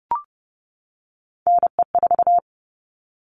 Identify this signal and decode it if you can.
Morse code, heard as a keyed tone, about 30 wpm, 700 Hz. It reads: NE4
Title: All Cut Numbers - 3-Digit 30wpm